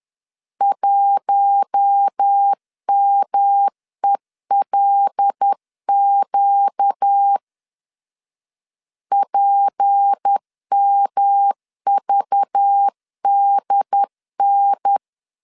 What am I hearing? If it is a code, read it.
Morse code, heard as a keyed tone, about 10 wpm, 800 Hz. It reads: 1MELQ PMVDN
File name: LEZIONI / 6 PKX0 / 6 P 10wpm 2432